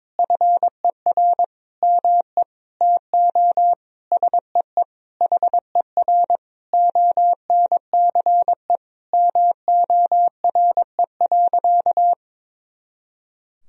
Morse code, heard as a keyed tone, about 22 wpm, 700 Hz. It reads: FER ME TO SEE HER ONCE MORE.
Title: Sentences from Top 500 Words and Abbreviations 22wpm